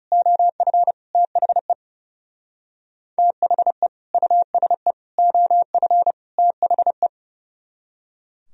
Morse code, heard as a keyed tone, about 39 wpm, 700 Hz. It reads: OF THE THE USE OF THE